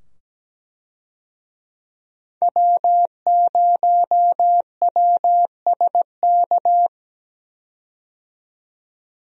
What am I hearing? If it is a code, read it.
Morse code, heard as a keyed tone, about 17 wpm, 700 Hz. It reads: W0WSK
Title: USA Call Signs - 5 Characters - 17wpm